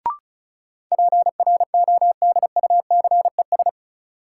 Morse code, heard as a keyed tone, about 36 wpm, 700 Hz. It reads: PRODUCES
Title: Sentences from Top 1000 Words - Review Part 6 - New Word Drill - 35wpm